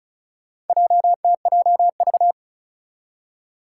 Morse code, heard as a keyed tone, about 36 wpm, 700 Hz. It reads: JTJV
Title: Four Challenging Letter-Number Combinations 35wpm